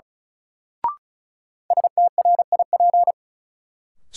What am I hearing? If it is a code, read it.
Morse code, heard as a keyed tone, about 36 wpm, 700 Hz. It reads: STRIP
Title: Sentences from Top 1600 Words - New Word Drill - 35wpm